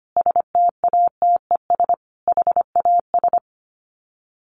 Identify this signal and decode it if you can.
Morse code, heard as a keyed tone, about 25 wpm, 700 Hz. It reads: STATES HAS
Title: Sets of 5 Words Encore 25wpm